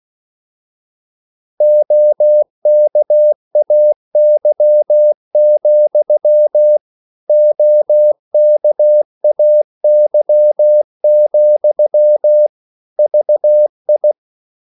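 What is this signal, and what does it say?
Morse code, heard as a keyed tone, about 16 wpm, 600 Hz. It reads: OKAY, OKAY, VI
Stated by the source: Slow (QRS) Morse code at 16 wpm, from CW med Gnister.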